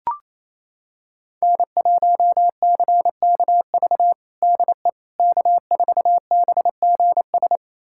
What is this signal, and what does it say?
Morse code, heard as a keyed tone, about 28 wpm, 700 Hz. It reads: N1CKV DE K4BGS
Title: USA Callsign DE Callsign - 5 Characters - 28wpm